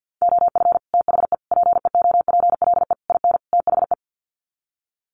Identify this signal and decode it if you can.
Morse code, heard as a keyed tone, about 49 wpm, 700 Hz. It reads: OF THE PEOPLE IN THE
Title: Sets of 5 Words Encore 50wpm